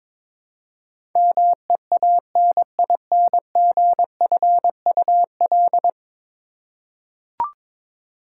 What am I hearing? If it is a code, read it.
Morse code, heard as a keyed tone, about 21 wpm, 700 Hz. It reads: MEANINGFUL
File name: practice / 22wpm / 10 Letter Words - First 3 Letters Given - 22wpm